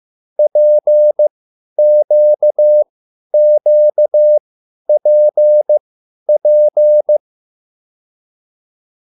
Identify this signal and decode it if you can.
Morse code, heard as a keyed tone, about 15 wpm, spaced as for 10 wpm, 600 Hz. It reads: PQQPP